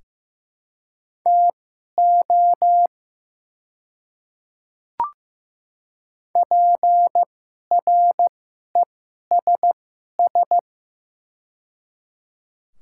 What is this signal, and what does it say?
Morse code, heard as a keyed tone, about 15 wpm, spaced as for 11 wpm, 700 Hz. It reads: TO PRESS
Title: 2x Character Spacing - Top 300 Words - 15wpm